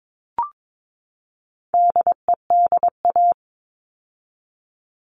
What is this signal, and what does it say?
Morse code, heard as a keyed tone, about 22 wpm, 700 Hz. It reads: DEDA
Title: All Cut Numbers - 4-Digit 22wpm